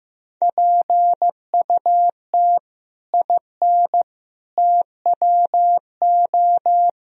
Morse code, heard as a keyed tone, about 16 wpm, 700 Hz. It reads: PUT IN TWO
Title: Sentences from Top 1000 Words - Review Part 2 - 15wpm